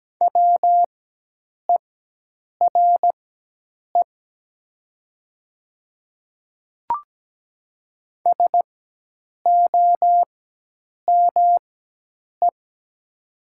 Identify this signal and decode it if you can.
Morse code, heard as a keyed tone, about 17 wpm, spaced as for 8 wpm, 690 Hz. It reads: WERE SOME
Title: 4x Character Spacing - Top 300 Words - 17wpm